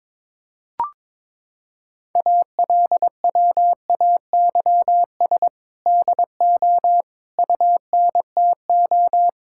Morse code, heard as a keyed tone, about 22 wpm, 700 Hz. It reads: ALWAYS DO UNTO
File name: practice / 22wpm / Sentences with 8-Word Phrases - 22wpm